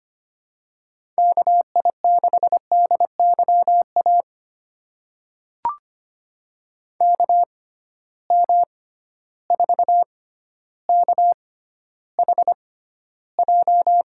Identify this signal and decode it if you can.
Morse code, heard as a keyed tone, about 25 wpm, spaced as for 13 wpm, 700 Hz. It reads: KI6DYA KM4KHJ
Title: USA Call Signs - 6 Characters - 6x Character Spacing - 25wpm